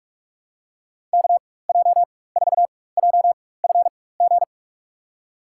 Morse code, heard as a keyed tone, about 43 wpm, spaced as for 21 wpm, 700 Hz. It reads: KJ4JFG
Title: USA Call Signs - 6 Characters - 4x Character Spacing - 45wpm